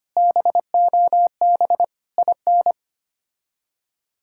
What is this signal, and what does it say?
Morse code, heard as a keyed tone, about 25 wpm, 700 Hz. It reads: BOB IN
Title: K1USN SST QSO Practice - 25wpm